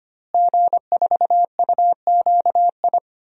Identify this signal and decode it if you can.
Morse code, heard as a keyed tone, about 25 wpm, 700 Hz. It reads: G4UQI
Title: DXCC-100 Call Signs - 5 Characters - Repeated 2x - 25wpm